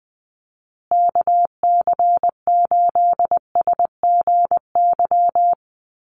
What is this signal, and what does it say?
Morse code, heard as a keyed tone, about 20 wpm, 700 Hz. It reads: KC8SGY
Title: USA Call Signs Repeated 2x 20wpm